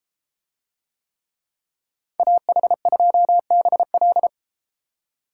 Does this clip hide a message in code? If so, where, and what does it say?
Yes: Morse code from 2.2 to 4.3 s, heard as a keyed tone, about 33 wpm, 700 Hz. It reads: AH2BL